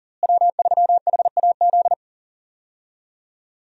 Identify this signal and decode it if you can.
Morse code, heard as a keyed tone, about 40 wpm, 700 Hz. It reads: W3HAZ
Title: USA Call Signs - 5 Characters - Repeated 3x - 40wpm